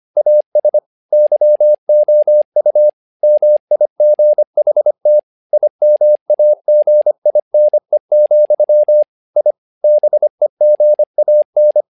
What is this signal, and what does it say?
Morse code, heard as a keyed tone, about 25 wpm, 600 Hz. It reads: AS YOU MIGHT IMAGINE, I BEGAN
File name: learning center / tower trouble / tower trouble 25wpm